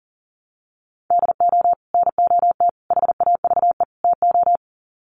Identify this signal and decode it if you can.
Morse code, heard as a keyed tone, about 40 wpm, 700 Hz. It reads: DO NOT HAVE TO
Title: Sets of 4 Words 40wpm